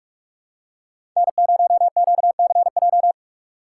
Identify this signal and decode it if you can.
Morse code, heard as a keyed tone, about 42 wpm, 700 Hz. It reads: N0QCJ
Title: USA Call Signs - 5 Characters - 6x Character Spacing - 45wpm